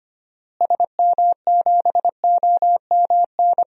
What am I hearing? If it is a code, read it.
Morse code, heard as a keyed tone, about 25 wpm, 690 Hz. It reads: SM7OMN
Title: DXCC-100 Call Signs - 6 Characters - Repeated 3x - 25wpm